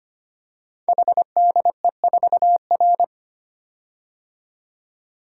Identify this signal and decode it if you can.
Morse code, heard as a keyed tone, about 25 wpm, 700 Hz. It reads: HDE4R
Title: Code Groups - 5 Letter:Numbers - 25wpm